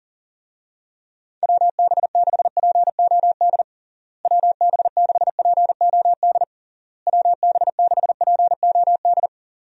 Morse code, heard as a keyed tone, about 37 wpm, 700 Hz. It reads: WB6POD WB6POD WB6POD
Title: USA Call Signs - 6 Characters - Repeated 3x - 40wpm